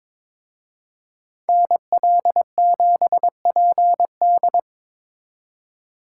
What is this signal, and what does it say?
Morse code, heard as a keyed tone, about 26 wpm, 700 Hz. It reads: NL7PD
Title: Speed Racing - USA Call Signs 22wpm